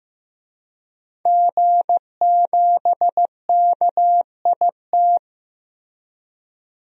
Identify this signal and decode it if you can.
Morse code, heard as a keyed tone, about 15 wpm, 700 Hz. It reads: G7KIT
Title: DXCC-100 Call Signs - 5 Characters - Repeated 2x - 15wpm